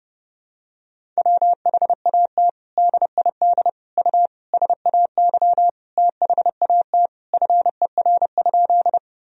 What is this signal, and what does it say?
Morse code, heard as a keyed tone, about 30 wpm, 700 Hz. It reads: WHAT DID U SAY THAT FER?
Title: Sentences from Top 200 Words and Abbreviations 30wpm